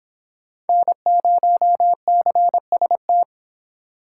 Morse code, heard as a keyed tone, about 26 wpm, 700 Hz. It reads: N0CST